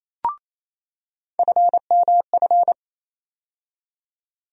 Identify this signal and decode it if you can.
Morse code, heard as a keyed tone, about 28 wpm, 700 Hz. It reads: FMF
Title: USA Call Sign Postfixes - 3 Letters 28wpm